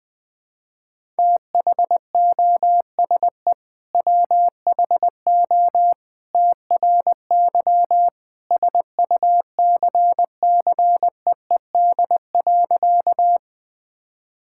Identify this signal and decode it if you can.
Morse code, heard as a keyed tone, about 20 wpm, 700 Hz. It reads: THOSE WHO TRY SUCCEED.
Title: Sentences with 2-Word Phrases - Spoken Beginning - 20wpm